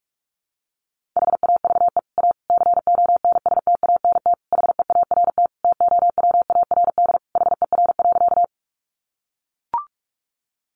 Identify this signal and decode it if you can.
Morse code, heard as a keyed tone, about 44 wpm, 700 Hz. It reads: HAVE A CONSTANT HEART TOWARD HER.
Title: Sentences from Top 1500 Words 45wpm